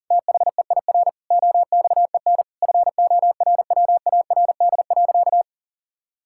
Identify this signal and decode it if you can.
Morse code, heard as a keyed tone, about 40 wpm, 700 Hz. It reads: THEIR OXEN FORWARD.
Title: Pangram Challenge v2 - 40wpm